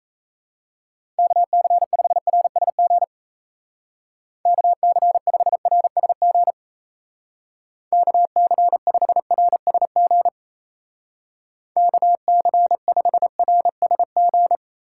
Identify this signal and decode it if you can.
Morse code, heard as a keyed tone, about 37 wpm, 700 Hz. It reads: KC5RSG KC5RSG KC5RSG KC5RSG